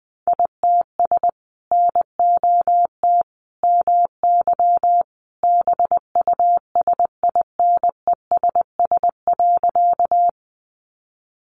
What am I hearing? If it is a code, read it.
Morse code, heard as a keyed tone, about 20 wpm, 700 Hz. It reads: ITS NOT MY BUSINESS.
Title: Sentences from Top 600 Words 20wpm